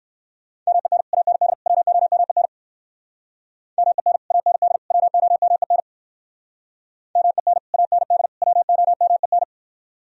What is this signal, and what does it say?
Morse code, heard as a keyed tone, about 58 wpm, 700 Hz. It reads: MEN AND WOMEN MEN AND WOMEN MEN AND WOMEN